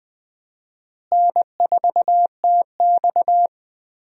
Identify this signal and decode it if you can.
Morse code, heard as a keyed tone, about 20 wpm, 700 Hz. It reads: N4TX
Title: USA Call Signs - 4 Characters - Repeated 3x - 20wpm